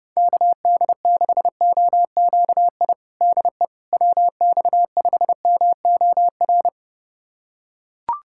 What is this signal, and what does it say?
Morse code, heard as a keyed tone, about 30 wpm, 700 Hz. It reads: KD6OQI DE WX5MOR
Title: USA Callsign DE Callsign - 6 Characters - 30wpm